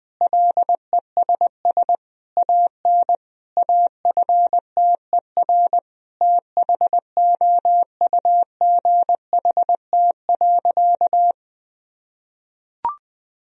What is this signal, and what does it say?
Morse code, heard as a keyed tone, about 20 wpm, 700 Hz. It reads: LESS AN AFTER THOUGHT.
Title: Sentences from Top 1000 Words - Review Part 2 - 20wpm